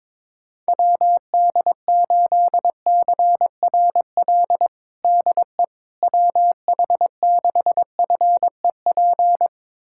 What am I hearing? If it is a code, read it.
Morse code, heard as a keyed tone, about 22 wpm, 700 Hz. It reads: WD8CRL DE WH6FEP
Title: USA Callsign DE Callsign - 6 Characters - 22wpm